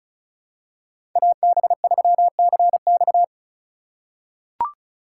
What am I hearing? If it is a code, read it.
Morse code, heard as a keyed tone, about 36 wpm, 690 Hz. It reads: AB3CX
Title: USA Call Signs - 5 Characters - 3x Character Spacing - 35wpm